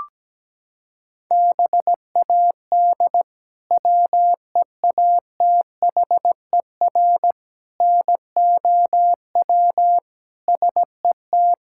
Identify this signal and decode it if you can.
Morse code, heard as a keyed tone, about 17 wpm, 700 Hz. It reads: BAD WEATHER NOW SET
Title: Sentences from Top 1000 Words - Review Part 5 - 17wpm